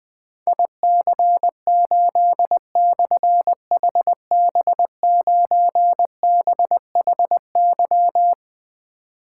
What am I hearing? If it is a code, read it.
Morse code, heard as a keyed tone, about 20 wpm, 700 Hz. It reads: IC8/HB9BHY
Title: Difficult Call Signs 20wpm